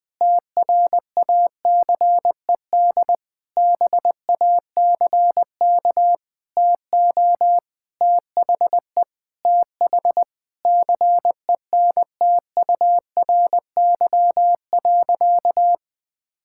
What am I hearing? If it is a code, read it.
Morse code, heard as a keyed tone, about 20 wpm, 700 Hz. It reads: TRACED BACK TO THE TH CENTURY.
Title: Sentences with 6-Word Phrases - 20wpm